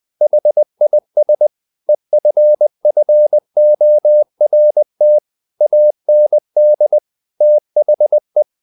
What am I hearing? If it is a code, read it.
Morse code, heard as a keyed tone, about 20 wpm, 600 Hz. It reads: HIS EFFORT AND THE
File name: the old man and the sea part 2 20wpm